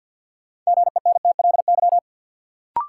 Morse code, heard as a keyed tone, about 46 wpm, 700 Hz. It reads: GENTLY
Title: Sentences from Top 1500 Words - New Word Drill Alt - 50wpm